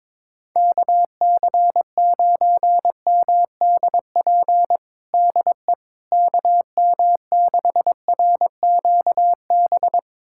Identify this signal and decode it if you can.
Morse code, heard as a keyed tone, about 21 wpm, 700 Hz. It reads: KC9MDP DE KM6RQB
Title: USA Callsign DE Callsign - 6 Characters - 22wpm